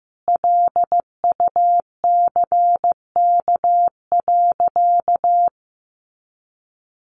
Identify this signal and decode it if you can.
Morse code, heard as a keyed tone, about 15 wpm, 700 Hz. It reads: LUCK.
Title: Sentences from Top 1500 Words 15wpm